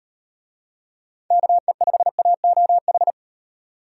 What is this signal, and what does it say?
Morse code, heard as a keyed tone, about 38 wpm, 700 Hz. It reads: KE5AOH